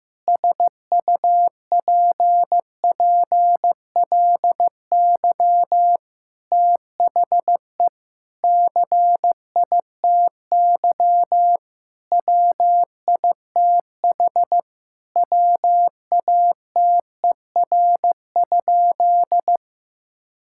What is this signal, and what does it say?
Morse code, heard as a keyed tone, about 15 wpm, 700 Hz. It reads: SUPPLY THE CITY WITH WATER?
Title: Sentences from Top 1000 Words - Review Part 5 - 15wpm